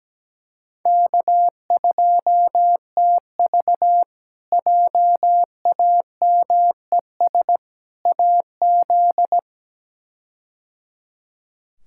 Morse code, heard as a keyed tone, about 17 wpm, 700 Hz. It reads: K2TV JAMES AZ
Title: K1USN SST QSO Practice - 17wpm